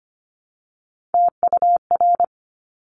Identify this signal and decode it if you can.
Morse code, heard as a keyed tone, about 25 wpm, 700 Hz. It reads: TUR
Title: Top Three Letter Combinations in Words 25wpm